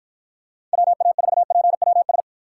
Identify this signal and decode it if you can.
Morse code, heard as a keyed tone, about 52 wpm, 700 Hz. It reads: WA4PWS